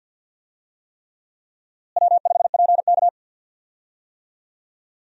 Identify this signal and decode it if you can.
Morse code, heard as a keyed tone, about 52 wpm, 700 Hz. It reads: W5PK
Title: USA Call Signs - 4 Characters - Repeated 3x - 50wpm